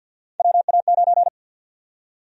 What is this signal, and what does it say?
Morse code, heard as a keyed tone, about 50 wpm, 700 Hz. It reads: WA9
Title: USA Call Sign Prefixes - 50wpm